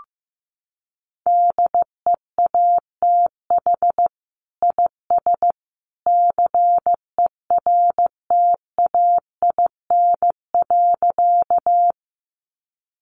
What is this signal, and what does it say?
Morse code, heard as a keyed tone, about 15 wpm, 700 Hz. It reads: DEATH IS CERTAIN.
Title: Sentences from Top 1100 Words 15wpm